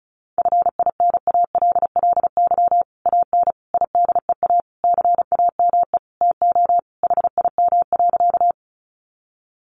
Morse code, heard as a keyed tone, about 34 wpm, 700 Hz. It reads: FINALLY AN IDEA CAME TO HIM.
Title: Sentences from Top 1000 Words 35wpm